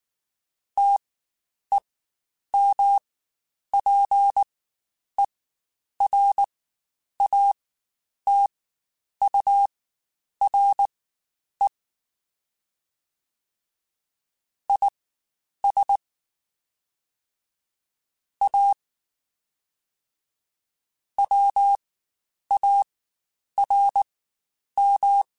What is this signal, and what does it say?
Morse code, heard as a keyed tone, about 19 wpm, spaced as for 9 wpm, 800 Hz. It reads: TEMPERATURE IS A WARM
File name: Spaced Characters / Spaced Characters-mp3s / Weather 4 10 sp @18